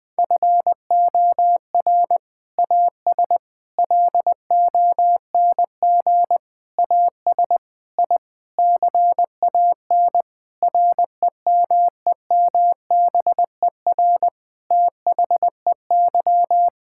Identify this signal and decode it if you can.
Morse code, heard as a keyed tone, about 20 wpm, 700 Hz. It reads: FOR AS LONG AS I CAN REMEMBER THEY
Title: Sentences with 7-Word Phrases - 20wpm